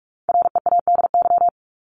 Morse code, heard as a keyed tone, about 44 wpm, 700 Hz. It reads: READY
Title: Top 500 Words 45wpm